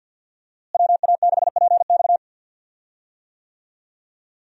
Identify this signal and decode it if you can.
Morse code, heard as a keyed tone, about 51 wpm, 700 Hz. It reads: WA6PX
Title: USA Call Signs - 5 Characters - Repeated 3x - 50wpm